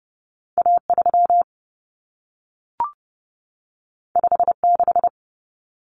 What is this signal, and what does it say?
Morse code, heard as a keyed tone, about 30 wpm, 700 Hz. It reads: A3 56
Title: Common Cut Numbers - 2-Digit 30wpm